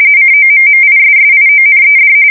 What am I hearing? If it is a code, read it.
RTTY, about 45.45 baud. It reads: COMMISSIONED
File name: fsk170 / battleshipnj